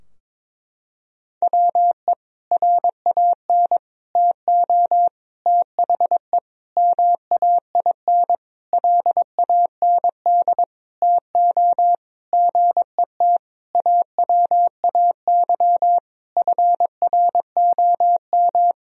Morse code, heard as a keyed tone, about 22 wpm, 700 Hz. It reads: WE RAN TO THE MAIN LAND TO GET AWAY FROM
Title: Sentences from Top 1000 Words - Review Part 2 - 22wpm